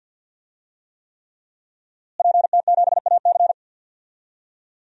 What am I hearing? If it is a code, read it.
Morse code, heard as a keyed tone, about 51 wpm, 700 Hz. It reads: PT7AC
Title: DXCC-100 Call Signs - 5 Characters - Repeated 3x - 50wpm